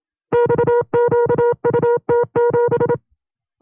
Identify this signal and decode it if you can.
Morse code, heard as a keyed tone, about 27 wpm, 480 Hz. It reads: XQUT7
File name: FreeFiles / CW-practice / 25WPM-side1